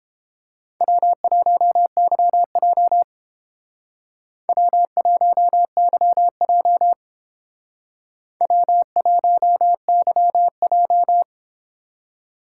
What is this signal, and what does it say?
Morse code, heard as a keyed tone, about 29 wpm, 700 Hz. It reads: W1YJ W1YJ W1YJ